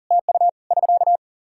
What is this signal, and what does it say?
Morse code, heard as a keyed tone, about 43 wpm, 700 Hz. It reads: TU <SK>
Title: Practice QSO - Exchange 4 - Ending - 40wpm